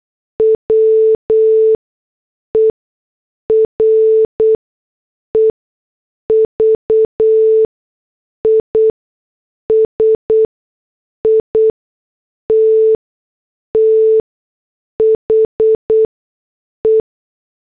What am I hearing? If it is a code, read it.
Morse code, heard as a keyed tone, about 8 wpm, 440 Hz. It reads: W E R E V I S I T T H E
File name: morse / headlines